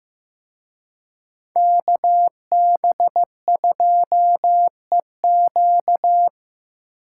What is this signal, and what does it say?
Morse code, heard as a keyed tone, about 18 wpm, 700 Hz. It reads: KB2EQ
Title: Speed Racing - USA 2x2 Call Signs 15wpm